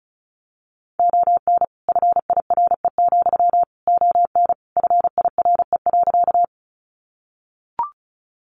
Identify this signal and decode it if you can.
Morse code, heard as a keyed tone, about 34 wpm, 700 Hz. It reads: ON FIRE, ON FIRE.
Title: Sentences from Top 400 Words 35wpm